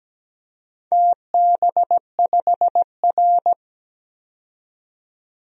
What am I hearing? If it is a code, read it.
Morse code, heard as a keyed tone, about 17 wpm, 700 Hz. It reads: TB5R
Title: DXCC-100 Call Signs - 4 Characters - 17wpm